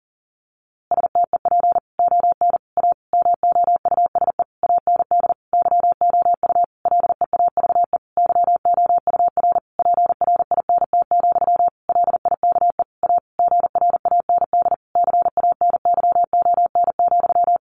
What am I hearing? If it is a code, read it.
Morse code, heard as a keyed tone, about 40 wpm, 700 Hz. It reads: STEP ON A MOUSE AND YOU LEAVE YOUR PRINT, LIKE A GRAND CANYON,
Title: A Sound Of Thunder 40wpm